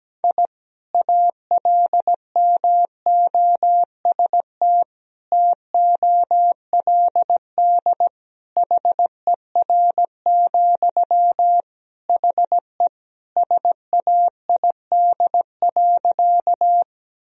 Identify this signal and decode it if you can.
Morse code, heard as a keyed tone, about 17 wpm, 700 Hz. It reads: I ALMOST TOLD HER, HE SAID.
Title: Sentences from Top 1000 Words - Review Part 4 - 17wpm